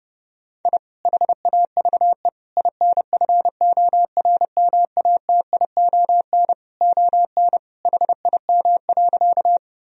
Morse code, heard as a keyed tone, about 30 wpm, 700 Hz. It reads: I HAVE INFORMATION ON HIM.
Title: Sentences from Top 1000 Words - Review Part 3 - 30wpm